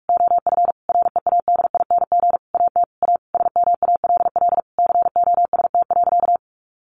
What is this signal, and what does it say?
Morse code, heard as a keyed tone, about 44 wpm, 700 Hz. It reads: OF READING AT A SMALL COST.
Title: Sentences from Top 900 Words 45wpm